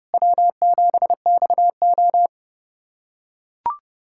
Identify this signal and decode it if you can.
Morse code, heard as a keyed tone, about 30 wpm, 700 Hz. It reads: W7XO
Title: USA Callsign DE Callsign - 4 Characters - 30wpm